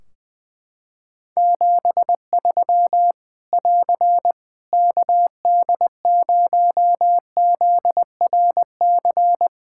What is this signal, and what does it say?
Morse code, heard as a keyed tone, about 20 wpm, 700 Hz. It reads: 73 <AR> KD0ZRC
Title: K1USN SST QSO Practice 20wpm